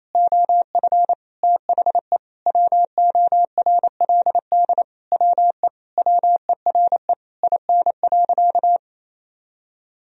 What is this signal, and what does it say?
Morse code, heard as a keyed tone, about 28 wpm, 700 Hz. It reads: OF THE WORLD WE WERE IN.
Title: Sentences from Top 1000 Words - Review Part 3 - 28wpm